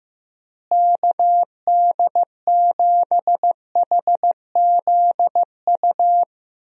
Morse code, heard as a keyed tone, about 15 wpm, 700 Hz. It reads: KD7HZU
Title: USA Call Signs - 6 Characters - Repeated 3x - 15wpm